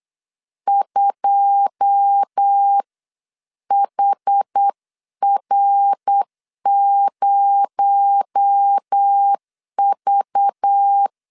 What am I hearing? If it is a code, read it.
Morse code, heard as a keyed tone, about 8 wpm, spaced as for 7 wpm, 800 Hz. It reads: 2HR0V